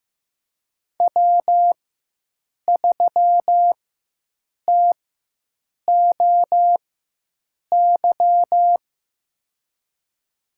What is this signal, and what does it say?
Morse code, heard as a keyed tone, about 15 wpm, spaced as for 7 wpm, 700 Hz. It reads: W3TOY
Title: USA Call Signs - 4x Character Spacing - 15wpm